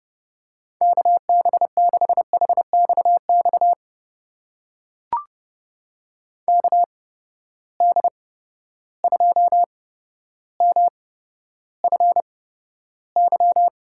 Morse code, heard as a keyed tone, about 30 wpm, spaced as for 13 wpm, 700 Hz. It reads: KB6HXX KD2MFY
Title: USA Call Signs - 6 Characters - 8x Character Spacing - 30wpm